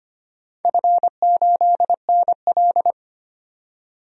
Morse code, heard as a keyed tone, about 25 wpm, 700 Hz. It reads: F8NL
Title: DXCC-100 Call Signs - 4 Characters - 25wpm